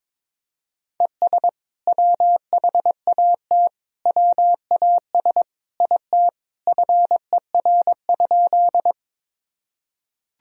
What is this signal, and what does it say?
Morse code, heard as a keyed tone, about 22 wpm, 700 Hz. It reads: ES WHAT WAS IT FER?
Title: Sentences from Top 100 Words and Abbreviations 22wpm